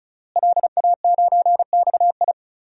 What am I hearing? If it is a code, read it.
Morse code, heard as a keyed tone, about 36 wpm, 700 Hz. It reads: LA9XI
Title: DXCC-100 Call Signs - 5 Characters - Repeated 3x - 35wpm